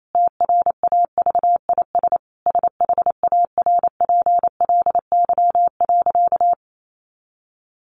Morse code, heard as a keyed tone, about 28 wpm, 700 Hz. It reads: TRAVIS SHARPLY.
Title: A Sound Of Thunder 28wpm